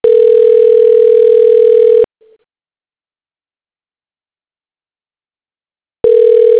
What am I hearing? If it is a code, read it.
Ringback tone (North America), 440 + 480 Hz.